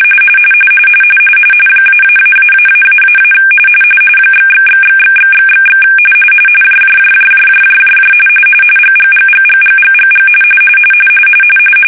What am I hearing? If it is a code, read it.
RTTY, about 45.45 baud. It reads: --------- ///////// JJJJJJJJJ GGGGGGGGG
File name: fsk-ascii850-pix / tiddly winks